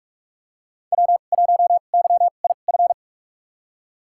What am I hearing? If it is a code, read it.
Morse code, heard as a keyed tone, about 47 wpm, spaced as for 33 wpm, 700 Hz. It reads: W1YIF
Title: USA Call Signs - 2x Character Spacing - 45wpm